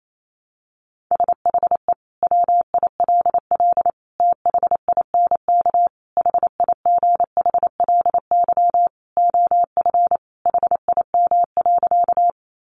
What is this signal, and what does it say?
Morse code, heard as a keyed tone, about 28 wpm, 700 Hz. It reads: SHE WILL THINK HIGHLY OF HIM.
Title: Sentences from Top 1100 Words 28wpm